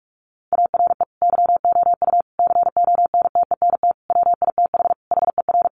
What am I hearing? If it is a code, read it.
Morse code, heard as a keyed tone, about 44 wpm, 700 Hz. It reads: ARE YOU CONTENT WITH HER
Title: Sentences from Top 900 Words 45wpm